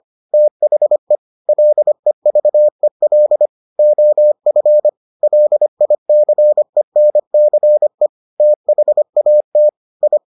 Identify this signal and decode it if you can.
Morse code, heard as a keyed tone, about 25 wpm, 600 Hz. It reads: THE LEVEL OF LICENCE THAT I